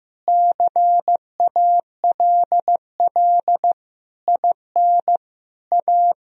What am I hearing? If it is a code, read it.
Morse code, heard as a keyed tone, about 15 wpm, 700 Hz. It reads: CALL IN A
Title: Sentences from Top 1000 Words - Review Part 1 - 15wpm